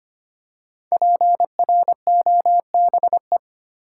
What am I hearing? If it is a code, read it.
Morse code, heard as a keyed tone, about 25 wpm, 700 Hz. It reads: PROBE
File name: 5 Letter Words - First 3 Letters Given - 25wpm